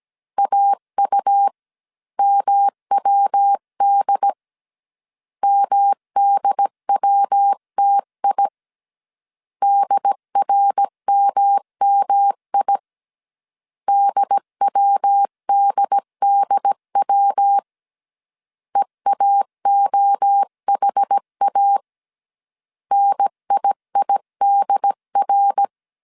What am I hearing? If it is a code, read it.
Morse code, heard as a keyed tone, about 17 wpm, 800 Hz. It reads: AUMWD MDWTI DRMMI DWDDW EAOHA NIIDR